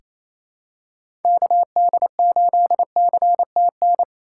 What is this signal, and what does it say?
Morse code, heard as a keyed tone, about 28 wpm, 700 Hz. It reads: KD8CTN
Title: USA Call Signs - 6 Characters - 3x Character Spacing - 28wpm